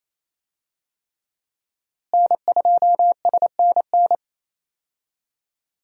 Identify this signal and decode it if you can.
Morse code, heard as a keyed tone, about 28 wpm, 700 Hz. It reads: N2SNN
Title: USA Call Signs - 5 Characters - Repeated 3x - 28wpm